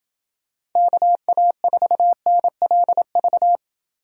Morse code, heard as a keyed tone, about 27 wpm, 700 Hz. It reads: KA4NLV